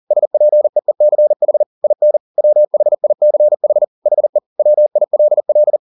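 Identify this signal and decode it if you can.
Morse code, heard as a keyed tone, about 40 wpm, 600 Hz. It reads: SPEECH IN WHICH HE WILL
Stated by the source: Morse code at 40 wpm, practice text, CW med Gnister.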